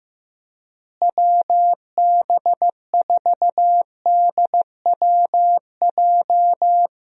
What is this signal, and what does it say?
Morse code, heard as a keyed tone, about 15 wpm, 700 Hz. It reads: WB4DWJ
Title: USA Call Signs - 6 Characters - Repeated 2x - 15wpm